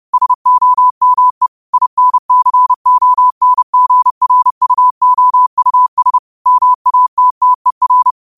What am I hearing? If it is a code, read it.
Morse code, heard as a keyed tone, about 30 wpm, 1000 Hz. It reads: SOME INCONGRUOUS MATTER
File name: qotdfiles / 30wpm